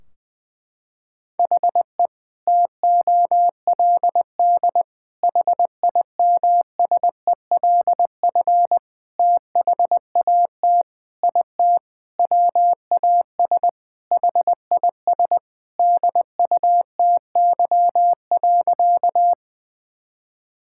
Morse code, heard as a keyed tone, about 20 wpm, 700 Hz. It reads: HE TOLD HIMSELF THAT IT WAS HIS DUTY.
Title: Sentences from Top 1600 Words 20wpm